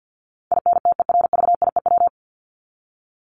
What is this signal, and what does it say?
Morse code, heard as a keyed tone, about 50 wpm, 700 Hz. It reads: INTERVIEW
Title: Top 1501-1750 Words 50wpm